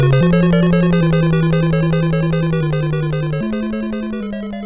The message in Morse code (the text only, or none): none